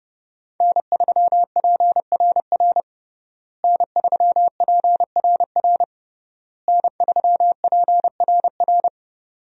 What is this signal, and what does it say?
Morse code, heard as a keyed tone, about 30 wpm, 700 Hz. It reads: N3PRR N3PRR N3PRR
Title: USA Call Signs - 5 Characters - Repeated 3x - 30wpm